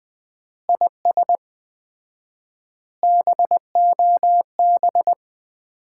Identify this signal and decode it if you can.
Morse code, heard as a keyed tone, about 21 wpm, 700 Hz. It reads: IS BOB
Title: Common QSO Phrases - 4x Word-Spacing - 20wpm